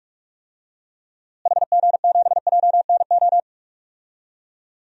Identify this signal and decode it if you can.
Morse code, heard as a keyed tone, about 46 wpm, 700 Hz. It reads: HG7JNO
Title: DXCC-100 Call Signs - 6 Characters - Repeated 2x - 45wpm